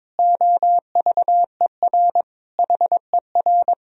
Morse code, heard as a keyed tone, about 22 wpm, 700 Hz. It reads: OVER HER
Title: Sentences from Top 1000 Words - Review Part 2 - 22wpm